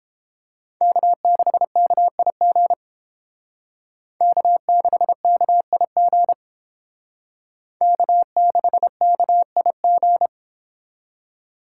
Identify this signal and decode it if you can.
Morse code, heard as a keyed tone, about 30 wpm, 700 Hz. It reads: K6KIG K6KIG K6KIG